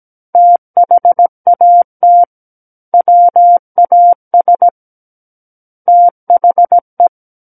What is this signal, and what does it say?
Morse code, heard as a keyed tone, about 17 wpm, 700 Hz. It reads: THAT WAS THE
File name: Book One / 08-Chapter 8 - 17 WPM